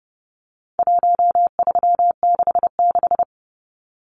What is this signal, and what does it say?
Morse code, heard as a keyed tone, about 30 wpm, 700 Hz. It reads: 1366
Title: Numbers - Contest Style - 4-Digit 30wpm